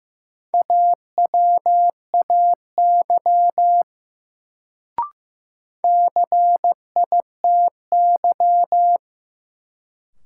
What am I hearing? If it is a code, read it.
Morse code, heard as a keyed tone, about 16 wpm, 700 Hz. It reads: AWAY CITY
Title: Sets of 2 Words - New Word Drill - 15wpm